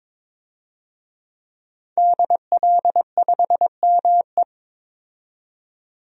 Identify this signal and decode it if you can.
Morse code, heard as a keyed tone, about 22 wpm, 700 Hz. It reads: DL5ME
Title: DXCC-100 Call Signs - 5 Characters - 22wpm